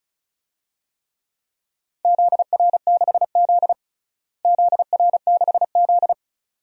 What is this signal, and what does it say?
Morse code, heard as a keyed tone, about 36 wpm, 700 Hz. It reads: ZR6Z ZR6Z
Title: DXCC-100 Call Signs - 4 Characters - Repeated 2x - 35wpm